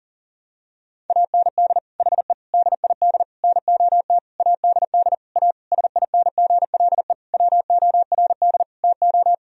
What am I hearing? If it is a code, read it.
Morse code, heard as a keyed tone, about 40 wpm, 700 Hz. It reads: AND HE DID NOT ADD A SINGLE WORD TO
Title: Sentences from Top 1000 Words - Review Part 1 - 40wpm